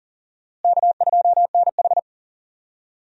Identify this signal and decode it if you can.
Morse code, heard as a keyed tone, about 40 wpm, 700 Hz. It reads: K2NH
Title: USA Call Signs - 4 Characters - Repeated 3x - 40wpm